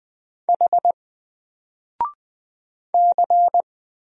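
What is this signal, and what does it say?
Morse code, heard as a keyed tone, about 20 wpm, 700 Hz. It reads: H C
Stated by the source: Single letter and number Morse drill at 20 wpm, from N4LCD.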